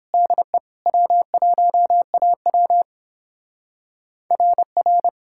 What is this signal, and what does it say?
Morse code, heard as a keyed tone, about 31 wpm, 700 Hz. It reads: DE W1AW RR